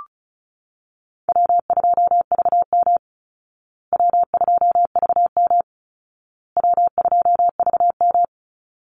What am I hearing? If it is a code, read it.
Morse code, heard as a keyed tone, about 34 wpm, 700 Hz. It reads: W2VM W2VM W2VM
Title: Easy Call Signs Repeated 3x - 35wpm